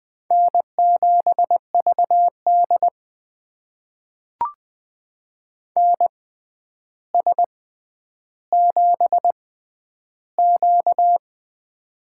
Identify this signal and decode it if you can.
Morse code, heard as a keyed tone, about 20 wpm, spaced as for 10 wpm, 700 Hz. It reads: N7VD NS7Q
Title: USA Call Signs - 4 Characters - 6x Character Spacing - 20wpm